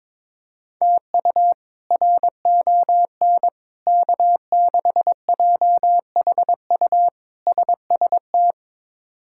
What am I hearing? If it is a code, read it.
Morse code, heard as a keyed tone, about 22 wpm, 700 Hz. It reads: TU RON K6JHU SST
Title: K1USN SST QSO Practice - 22wpm